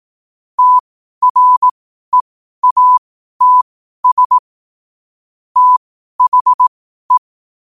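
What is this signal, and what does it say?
Morse code, heard as a keyed tone, about 18 wpm, spaced as for 12 wpm, 1000 Hz. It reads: TREATS THE
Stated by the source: Morse practice sent at 12 wpm.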